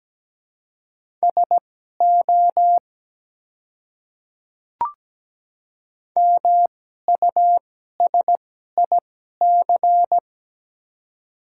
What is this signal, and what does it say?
Morse code, heard as a keyed tone, about 17 wpm, spaced as for 12 wpm, 700 Hz. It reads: SO MUSIC
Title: 2x Character Spacing - Top 300 Words - 17wpm